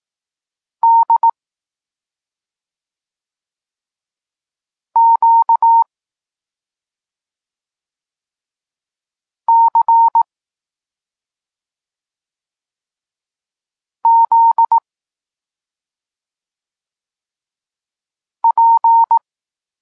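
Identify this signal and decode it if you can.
Morse code, heard as a keyed tone, about 18 wpm, 920 Hz. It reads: D Q C Z P
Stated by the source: Polish Morse lesson teaching the letter C, from SP9MOA Niepołomicki Klub Krótkofalowców.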